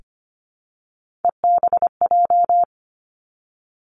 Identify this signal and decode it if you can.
Morse code, heard as a keyed tone, about 25 wpm, 700 Hz. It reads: EBJ
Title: Three Challenging Letter-Number Combinations 25wpm